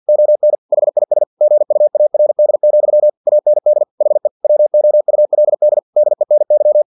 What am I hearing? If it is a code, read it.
Morse code, heard as a keyed tone, about 50 wpm, 600 Hz. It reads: ON HIS GUARD, AND HE WOULD DENY
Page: Never 49wpm